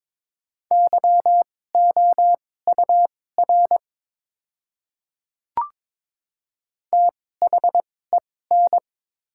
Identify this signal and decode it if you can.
Morse code, heard as a keyed tone, about 22 wpm, spaced as for 16 wpm, 700 Hz. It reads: YOUR THEN
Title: 2x Character Spacing - Top 100 Words - 22wpm